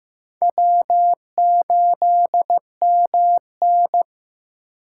Morse code, heard as a keyed tone, about 15 wpm, 700 Hz. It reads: W8MN
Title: USA Call Signs - 4 Characters - 4x Character Spacing - 15wpm